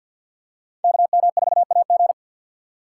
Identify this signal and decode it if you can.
Morse code, heard as a keyed tone, about 49 wpm, 700 Hz. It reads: KM4AG
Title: USA Call Signs - 4x Character Spacing - 50wpm